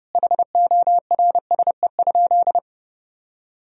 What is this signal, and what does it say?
Morse code, heard as a keyed tone, about 30 wpm, 700 Hz. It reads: HORSE?
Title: 4x Word-Spacing - Sentences from Top 500 Words - 30wpm